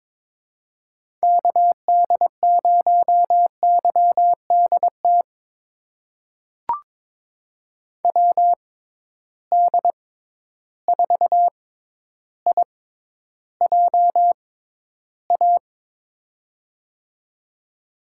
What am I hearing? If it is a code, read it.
Morse code, heard as a keyed tone, about 22 wpm, spaced as for 11 wpm, 700 Hz. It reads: KD0YDT WD4IJA
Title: USA Call Signs - 6 Characters - 6x Character Spacing - 22wpm